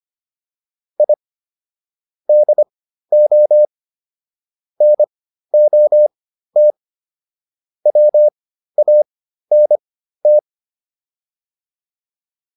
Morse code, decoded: I DO NOT WANT

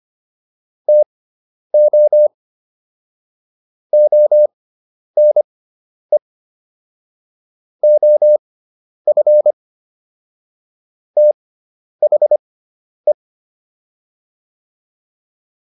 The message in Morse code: TO ONE OF THE

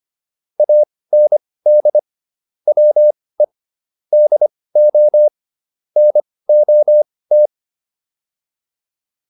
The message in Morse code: AND WE DO NOT

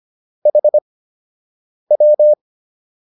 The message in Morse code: HW

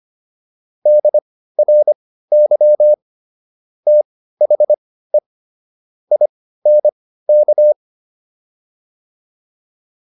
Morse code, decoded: DRY THE INK

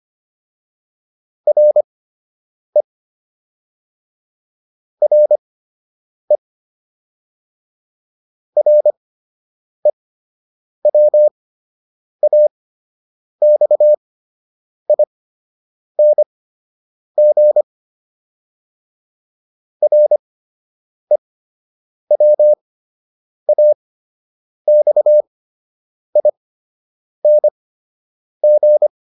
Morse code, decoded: RE RE REWAXING REWAXING

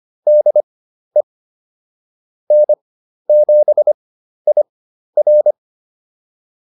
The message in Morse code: DE N7IR

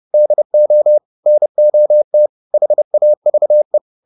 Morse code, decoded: DO NOT HAVE

